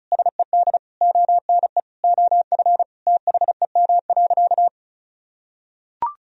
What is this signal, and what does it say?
Morse code, heard as a keyed tone, about 35 wpm, 700 Hz. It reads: SED ONE OF THEM.
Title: Sentences from Top 300 Words and Abbreviations 35wpm